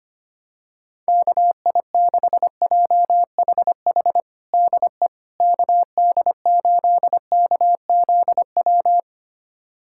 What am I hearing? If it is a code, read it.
Morse code, heard as a keyed tone, about 25 wpm, 700 Hz. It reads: KI6JHH DE KD8KZW
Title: USA Callsign DE Callsign 25wpm